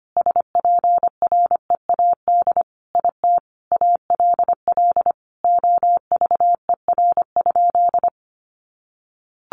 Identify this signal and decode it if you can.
Morse code, heard as a keyed tone, about 25 wpm, 700 Hz. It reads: SPREAD IT ALL OVER?
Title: Sentences from Top 1200 Words 25wpm